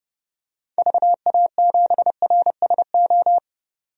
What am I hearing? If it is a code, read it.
Morse code, heard as a keyed tone, about 30 wpm, 700 Hz. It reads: VA7RSO
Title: DXCC-100 Call Signs - 6 Characters - 30wpm